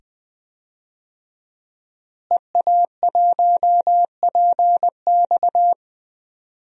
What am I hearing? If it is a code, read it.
Morse code, heard as a keyed tone, about 19 wpm, 700 Hz. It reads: EA1PX
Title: DXCC-100 Call Signs - 5 Characters - Repeated 2x - 20wpm